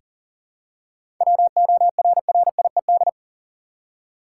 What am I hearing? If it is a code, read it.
Morse code, heard as a keyed tone, about 40 wpm, 700 Hz. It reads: WORRIED
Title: Seven Letter Words - 40wpm